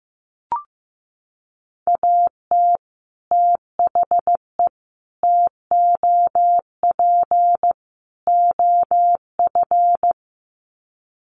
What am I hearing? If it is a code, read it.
Morse code, heard as a keyed tone, about 15 wpm, 700 Hz. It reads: AT THE TOP OF
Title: Sets of 4 Words 15wpm